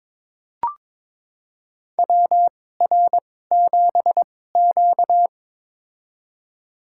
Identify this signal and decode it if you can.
Morse code, heard as a keyed tone, about 22 wpm, spaced as for 16 wpm, 700 Hz. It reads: WR7Q
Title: USA Call Signs - 4 Characters - 2x Character Spacing - 22wpm